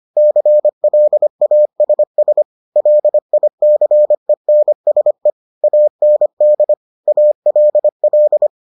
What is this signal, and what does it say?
Morse code, heard as a keyed tone, about 25 wpm, 600 Hz. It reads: CLASS LICENSE AND ALL